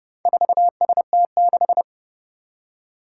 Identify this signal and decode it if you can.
Morse code, heard as a keyed tone, about 30 wpm, 700 Hz. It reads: 4ST6
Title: Four Challenging Letter-Number Combinations 30wpm